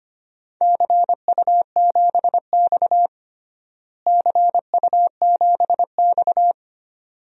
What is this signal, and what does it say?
Morse code, heard as a keyed tone, about 25 wpm, 700 Hz. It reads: CU7X CU7X
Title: DXCC-100 Call Signs - 4 Characters - Repeated 2x - 25wpm